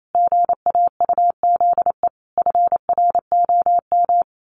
Morse code, heard as a keyed tone, about 28 wpm, 700 Hz. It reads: GAUZE FROM
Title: A Sound Of Thunder 28wpm